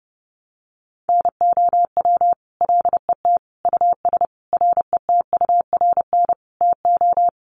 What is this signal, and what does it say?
Morse code, heard as a keyed tone, about 30 wpm, 700 Hz. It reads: NOW LET US RETURN TO
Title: Sentences from Top 800 Words 30wpm